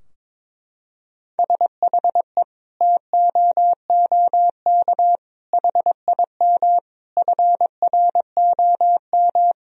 Morse code, heard as a keyed tone, about 22 wpm, 700 Hz. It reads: SHE TOOK HIM FROM
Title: Sentences from Top 1000 Words - Review Part 4 - 22wpm